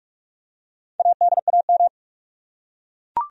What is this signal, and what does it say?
Morse code, heard as a keyed tone, about 42 wpm, 700 Hz. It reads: ADAM
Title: USA Common Names - Max 4 Letters - 45wpm